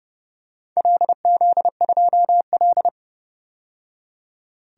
Morse code, heard as a keyed tone, about 30 wpm, 700 Hz. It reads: LZ2L
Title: DXCC-100 Call Signs - 4 Characters - 30wpm